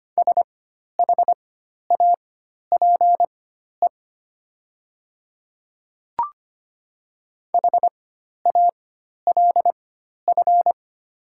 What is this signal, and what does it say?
Morse code, heard as a keyed tone, about 25 wpm, spaced as for 12 wpm, 700 Hz. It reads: SHAPE HALF
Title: 4x Character Spacing - Top 500 Words - 25wpm